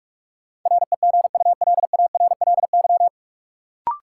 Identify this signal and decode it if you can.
Morse code, heard as a keyed tone, about 46 wpm, 700 Hz. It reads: REGULARLY
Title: Sentences from Top 1400 Words - New Word Drill Alt - 45wpm